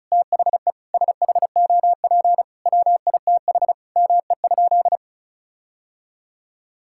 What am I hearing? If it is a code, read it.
Morse code, heard as a keyed tone, about 36 wpm, 700 Hz. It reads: THE SHOP WITH ME?
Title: Sentences from Top 1600 Words 35wpm